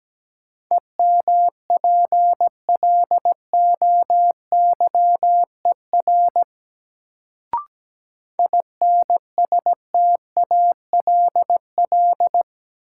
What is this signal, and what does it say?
Morse code, heard as a keyed tone, about 17 wpm, 700 Hz. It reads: EMPLOYER INSTALL
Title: Sentences from Top 1600 Words - New Word Drill - 17wpm